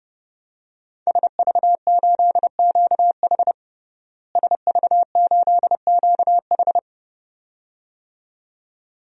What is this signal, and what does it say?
Morse code, heard as a keyed tone, about 30 wpm, 700 Hz. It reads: SV8QH SV8QH
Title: DXCC-100 Call Signs - 5 Characters - Repeated 2x - 30wpm